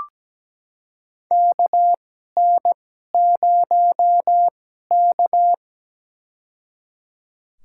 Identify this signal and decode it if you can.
Morse code, heard as a keyed tone, about 17 wpm, spaced as for 12 wpm, 700 Hz. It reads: KN0K
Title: USA Call Signs - 4 Characters - 2x Character Spacing - 17wpm